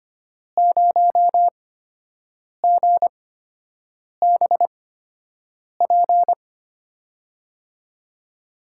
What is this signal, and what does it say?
Morse code, heard as a keyed tone, about 25 wpm, spaced as for 7 wpm, 700 Hz. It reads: 0GBP
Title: USA Call Signs - 6 Characters - 8x Character Spacing - 25wpm